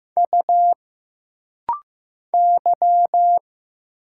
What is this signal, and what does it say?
Morse code, heard as a keyed tone, about 15 wpm, 700 Hz. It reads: U Y
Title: Single Letter-Number Alt - Rapid-Fire 15wpm